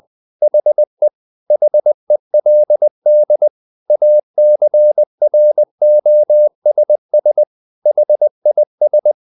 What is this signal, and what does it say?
Morse code, heard as a keyed tone, about 20 wpm, 600 Hz. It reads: HE HELD ACROSS HIS
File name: the old man and the sea / the old man and the sea part 1 20wpm